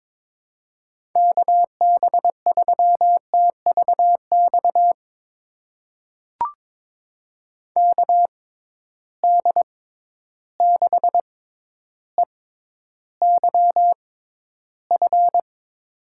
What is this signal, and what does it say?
Morse code, heard as a keyed tone, about 22 wpm, spaced as for 11 wpm, 700 Hz. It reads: KB3TVX KD6EYF
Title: USA Call Signs - 6x Character Spacing - 22wpm